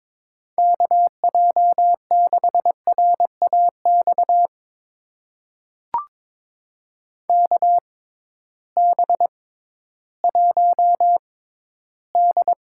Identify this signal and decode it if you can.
Morse code, heard as a keyed tone, about 22 wpm, spaced as for 13 wpm, 700 Hz. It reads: KJ6RAX KB1D
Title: USA Call Signs - 6 Characters - 6x Character Spacing - 22wpm